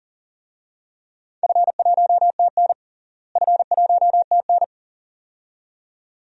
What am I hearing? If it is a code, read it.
Morse code, heard as a keyed tone, about 40 wpm, 700 Hz. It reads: F1TN F1TN
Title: DXCC-100 Call Signs - 4 Characters - Repeated 2x - 40wpm